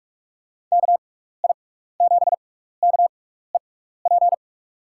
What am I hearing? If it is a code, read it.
Morse code, heard as a keyed tone, about 46 wpm, spaced as for 16 wpm, 700 Hz. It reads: KI7KEP
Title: USA Call Signs - 6 Characters - 6x Character Spacing - 45wpm